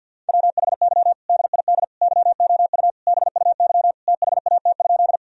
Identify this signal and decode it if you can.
Morse code, heard as a keyed tone, about 52 wpm, 700 Hz. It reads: WHY DID YOU BUY THAT?
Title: Sentences from Top 1000 Words - Review Part 4 - 50wpm